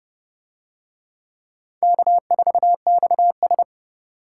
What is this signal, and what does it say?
Morse code, heard as a keyed tone, about 26 wpm, 700 Hz. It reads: K4XS